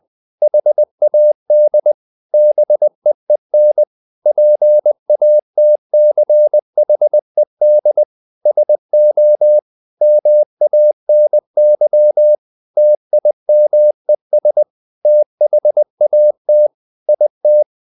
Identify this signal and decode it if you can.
Morse code, heard as a keyed tone, about 20 wpm, 600 Hz. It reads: HAD BEEN PATCHED SO MANY TIMES THAT IT
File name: the old man and the sea / the old man and the sea part 1 20wpm